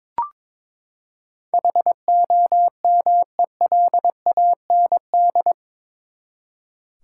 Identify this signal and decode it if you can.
Morse code, heard as a keyed tone, about 22 wpm, 700 Hz. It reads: HOMELAND
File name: practice / 22wpm / Eight Letter Words - 22wpm